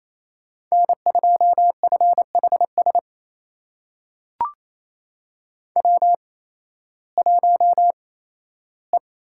Morse code, heard as a keyed tone, about 28 wpm, spaced as for 15 wpm, 700 Hz. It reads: N2FHS W1E
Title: USA Call Signs - 5 Characters - 8x Character Spacing - 28wpm